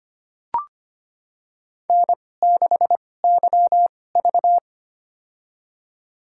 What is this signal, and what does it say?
Morse code, heard as a keyed tone, about 25 wpm, spaced as for 18 wpm, 700 Hz. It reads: N6YV
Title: USA Call Signs - 4 Characters - 2x Character Spacing - 25wpm